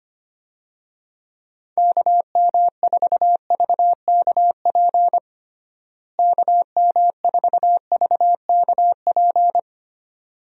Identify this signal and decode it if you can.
Morse code, heard as a keyed tone, about 25 wpm, 700 Hz. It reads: KM4VKP KM4VKP
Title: USA Call Signs - 6 Characters - Repeated 2x - 25wpm